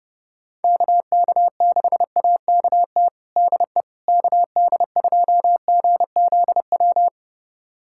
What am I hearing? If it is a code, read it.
Morse code, heard as a keyed tone, about 32 wpm, 700 Hz. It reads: KK6AKT DE KD2GZW
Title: USA Callsign DE Callsign - 6 Characters - 30wpm